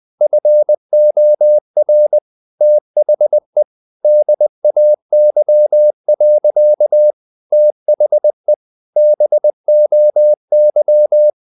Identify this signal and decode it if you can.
Morse code, heard as a keyed tone, about 20 wpm, 600 Hz. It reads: FOR THE DAY. THE BOY
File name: the old man and the sea / the old man and the sea part 1 20wpm